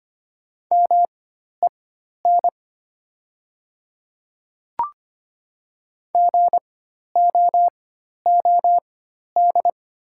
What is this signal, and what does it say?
Morse code, heard as a keyed tone, about 25 wpm, spaced as for 12 wpm, 700 Hz. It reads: MEN GOOD
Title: 4x Character Spacing - Top 200 Words - 25wpm